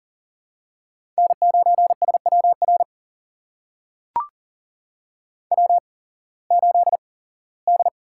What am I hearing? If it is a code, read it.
Morse code, heard as a keyed tone, about 40 wpm, spaced as for 21 wpm, 700 Hz. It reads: N9SWR W8D
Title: USA Call Signs - 5 Characters - 8x Character Spacing - 40wpm